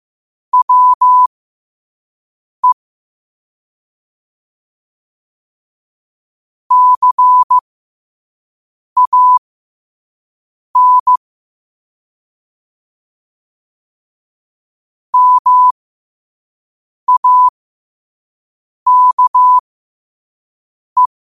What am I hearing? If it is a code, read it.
Morse code, heard as a keyed tone, about 15 wpm, spaced as for 5 wpm, 1000 Hz. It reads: WE CAN MAKE
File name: qotdfiles / 5wpm